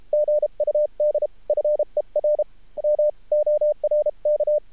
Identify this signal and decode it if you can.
Morse code, heard as a keyed tone, about 33 wpm, spaced as for 29 wpm, 600 Hz. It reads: GUD FER MORK
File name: archived / CRSnet